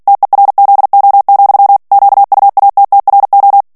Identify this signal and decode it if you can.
Morse code, heard as a keyed tone, about 48 wpm, 800 Hz. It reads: TERZO, QUATTRO
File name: ITALIEN texte clair 2 vitesse 45 mots minute